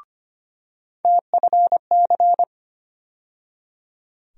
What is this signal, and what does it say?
Morse code, heard as a keyed tone, about 25 wpm, 700 Hz. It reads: TFC
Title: USA Call Sign Postfixes - 3 Letters 25wpm